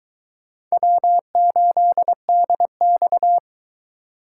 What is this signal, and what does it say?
Morse code, heard as a keyed tone, about 23 wpm, 700 Hz. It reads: W8DX